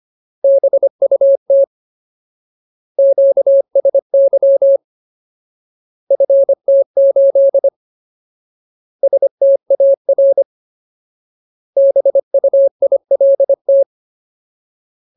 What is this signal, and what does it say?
Morse code, heard as a keyed tone, about 25 wpm, 550 Hz. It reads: BUT QSY FT8 STAR BUILT
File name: Ultimate Word List / 25WPM Ultimate